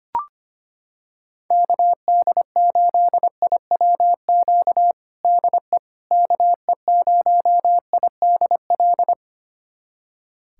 Morse code, heard as a keyed tone, about 25 wpm, 700 Hz. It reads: KD8IWQ DE KE0IDL
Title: USA Callsign DE Callsign - 6 Characters - 25wpm